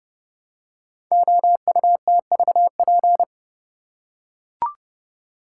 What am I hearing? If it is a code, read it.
Morse code, heard as a keyed tone, about 29 wpm, 700 Hz. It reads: OUTVP
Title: Code Groups - 5 Letter:Numbers - 30wpm